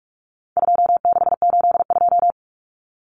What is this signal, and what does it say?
Morse code, heard as a keyed tone, about 44 wpm, 700 Hz. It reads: 2682
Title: Common Cut Numbers - 4-Digit 45wpm